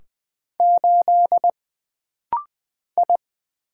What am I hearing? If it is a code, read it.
Morse code, heard as a keyed tone, about 20 wpm, 700 Hz. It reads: 8 I
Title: Single Letter-Number - Rapid-Fire 20wpm